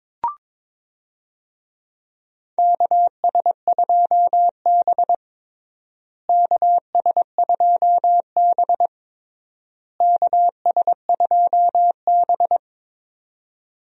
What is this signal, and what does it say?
Morse code, heard as a keyed tone, about 22 wpm, 700 Hz. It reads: KS2B KS2B KS2B
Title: USA Call Signs - 4 Characters - Repeated 3x - 22wpm